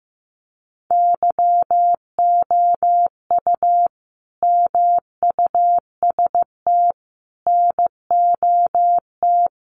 Morse code, heard as a keyed tone, about 15 wpm, 700 Hz. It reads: YOU MUST NOT
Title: Sentences from Top 1300 Words 15wpm